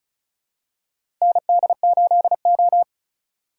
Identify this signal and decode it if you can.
Morse code, heard as a keyed tone, about 36 wpm, 700 Hz. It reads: ND8O
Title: USA Call Signs - 2x Character Spacing - 35wpm